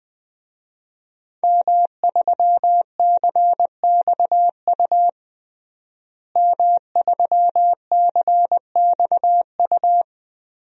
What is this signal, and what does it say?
Morse code, heard as a keyed tone, about 20 wpm, 700 Hz. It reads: M3CXU M3CXU
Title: DXCC-100 Call Signs - 5 Characters - Repeated 3x - 20wpm